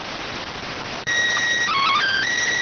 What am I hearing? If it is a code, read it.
SSTV (slow-scan television), PD 290.